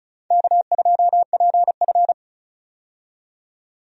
Morse code, heard as a keyed tone, about 34 wpm, 700 Hz. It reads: K2PF
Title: USA Call Signs - 4 Characters - 35wpm